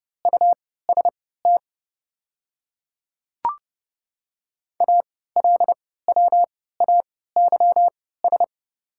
Morse code, heard as a keyed tone, about 30 wpm, spaced as for 17 wpm, 690 Hz. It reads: UST ALWAYS
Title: 3x Character Spacing - Top 300 Words - 30wpm